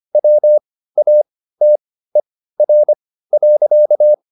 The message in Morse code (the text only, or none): WATER.